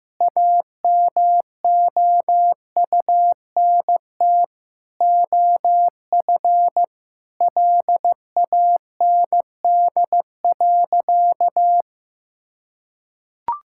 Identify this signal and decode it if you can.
Morse code, heard as a keyed tone, about 15 wpm, 700 Hz. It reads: AMOUNT OF LAND.
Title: Sentences from Top 1000 Words - Review Part 6 - 15wpm